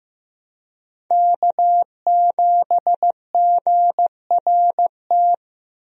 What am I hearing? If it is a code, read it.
Morse code, heard as a keyed tone, about 15 wpm, 700 Hz. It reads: K7GRT
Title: USA Call Signs - 5 Characters - Repeated 3x - 15wpm